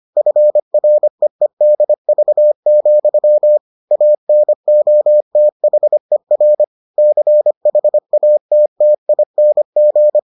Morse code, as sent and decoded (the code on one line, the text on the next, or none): ..-. .-. . -.. ...- --..-- / .- -. --- - .... .-. / -.-. .... .- - - .. -. --.
FREEDV, ANOTHER CHATTING